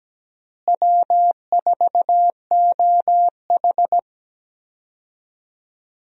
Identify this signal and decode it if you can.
Morse code, heard as a keyed tone, about 17 wpm, 700 Hz. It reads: W4OH
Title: USA Call Signs - 4 Characters - Repeated 3x - 17wpm